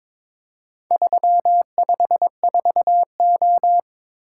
Morse code, heard as a keyed tone, about 22 wpm, 700 Hz. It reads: 354O
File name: practice / 22wpm / Four Letter-Number Combinations - 22wpm